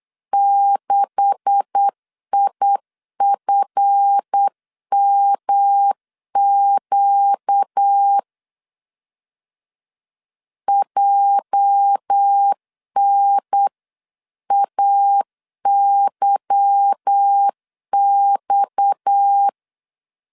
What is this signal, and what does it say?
Morse code, heard as a keyed tone, about 8 wpm, 800 Hz. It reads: 6IFMQ JNAYX
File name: 8wpm